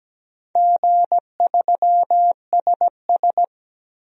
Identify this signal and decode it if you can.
Morse code, heard as a keyed tone, about 17 wpm, 700 Hz. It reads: G3SS
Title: DXCC-100 Call Signs - 4 Characters - 17wpm